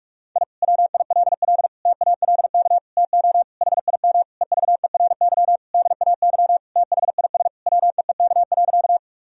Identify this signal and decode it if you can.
Morse code, heard as a keyed tone, about 46 wpm, 700 Hz. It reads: I WILL TALK TO HIM EVERY DAY THIS WEEK.
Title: Sentences from Top 1000 Words - Review Part 2 - 45wpm